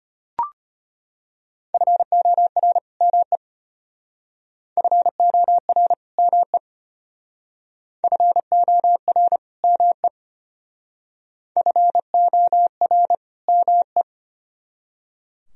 Morse code, heard as a keyed tone, about 31 wpm, 700 Hz. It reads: FOR ME FOR ME FOR ME FOR ME